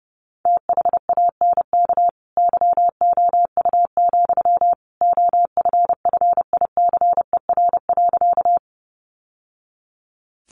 Morse code, heard as a keyed tone, about 30 wpm, 700 Hz. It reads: THANK YOU, OFFICER.
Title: Sentences from Top 1200 Words 30wpm